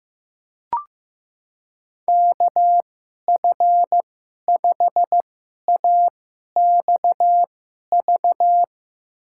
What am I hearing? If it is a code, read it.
Morse code, heard as a keyed tone, about 15 wpm, spaced as for 11 wpm, 700 Hz. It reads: KF5AXV
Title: USA Call Signs - 6 Characters - 2x Character Spacing - 15wpm